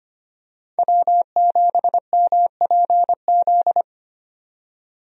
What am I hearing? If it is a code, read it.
Morse code, heard as a keyed tone, about 25 wpm, 700 Hz. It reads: W7MPZ
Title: USA Call Signs - 5 Characters - 25wpm